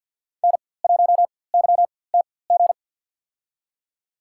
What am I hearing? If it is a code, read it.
Morse code, heard as a keyed tone, about 47 wpm, spaced as for 23 wpm, 700 Hz. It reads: N1YTG
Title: USA Call Signs - 5 Characters - 4x Character Spacing - 50wpm